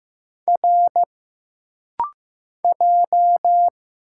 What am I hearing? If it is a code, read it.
Morse code, heard as a keyed tone, about 15 wpm, 700 Hz. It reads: R J
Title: Single Letter-Number Alt - Rapid-Fire 15wpm